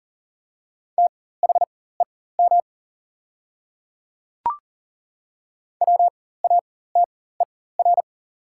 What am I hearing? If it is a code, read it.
Morse code, heard as a keyed tone, about 40 wpm, spaced as for 19 wpm, 700 Hz. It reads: THEM WATER
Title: 4x Character Spacing - Top 100 Words - 40wpm